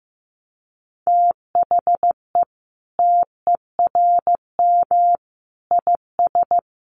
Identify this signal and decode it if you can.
Morse code, heard as a keyed tone, about 15 wpm, 700 Hz. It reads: THE TERM IS
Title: Sentences from Top 900 Words 15wpm